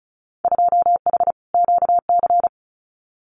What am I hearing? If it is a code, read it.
Morse code, heard as a keyed tone, about 34 wpm, 700 Hz. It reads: 2H QC
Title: Winter Field Day Exchanges 35wpm